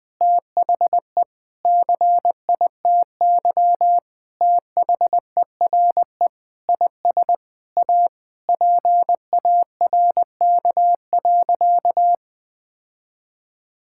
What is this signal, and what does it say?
Morse code, heard as a keyed tone, about 20 wpm, 700 Hz. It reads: THE CITY THERE IS A PARK.
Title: Sentences with 5-Word Phrases - 20wpm